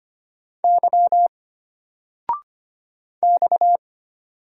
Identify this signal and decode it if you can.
Morse code, heard as a keyed tone, about 25 wpm, 700 Hz. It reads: Y X
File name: Single-Letter-Number-25wpm